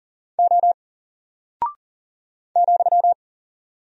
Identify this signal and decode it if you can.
Morse code, heard as a keyed tone, about 41 wpm, 700 Hz. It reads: O ,
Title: Single Character (Letter:Number:Punctuation:Prosign) 40wpm